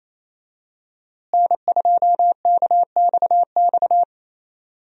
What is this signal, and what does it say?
Morse code, heard as a keyed tone, about 28 wpm, 700 Hz. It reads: N2KXX
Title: USA Call Signs - 5 Characters - Repeated 3x - 28wpm